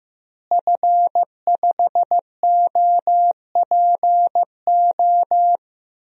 Morse code, heard as a keyed tone, about 16 wpm, 700 Hz. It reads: F5OPO
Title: DXCC-100 Call Signs - 5 Characters - Repeated 3x - 15wpm